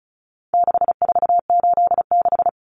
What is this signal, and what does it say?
Morse code, heard as a keyed tone, about 34 wpm, 700 Hz. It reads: 6486
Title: Numbers - 4-Digit 35wpm